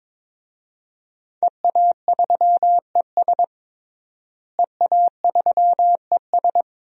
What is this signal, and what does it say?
Morse code, heard as a keyed tone, about 22 wpm, 700 Hz. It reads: EA3ES EA3ES
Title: DXCC-100 Call Signs - 5 Characters - Repeated 2x - 22wpm